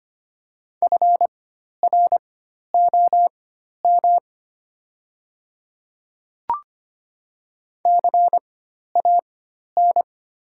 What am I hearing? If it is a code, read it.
Morse code, heard as a keyed tone, about 25 wpm, spaced as for 12 wpm, 700 Hz. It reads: FROM CAN
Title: 4x Character Spacing - Top 100 Words - 25wpm